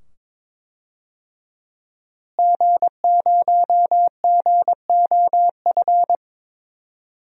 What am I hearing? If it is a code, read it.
Morse code, heard as a keyed tone, about 22 wpm, 700 Hz. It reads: G0GOF
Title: DXCC-100 Call Signs - 5 Characters - Repeated 2x - 22wpm